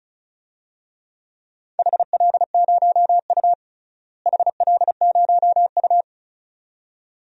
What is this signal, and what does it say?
Morse code, heard as a keyed tone, about 36 wpm, 700 Hz. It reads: HL0U HL0U
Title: DXCC-100 Call Signs - 4 Characters - Repeated 2x - 35wpm